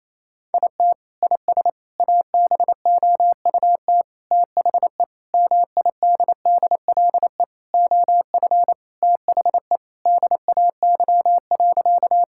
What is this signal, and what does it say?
Morse code, heard as a keyed tone, about 27 wpm, 700 Hz. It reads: IT IS ABOUT THE MIDDLE OF THE DAY.
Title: Sentences from Top 1000 Words - Review Part 6 - 28wpm